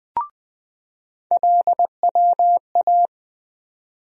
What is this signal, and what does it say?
Morse code, heard as a keyed tone, about 21 wpm, 700 Hz. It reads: LWA
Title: USA Call Sign Postfixes - 3 Letters 20wpm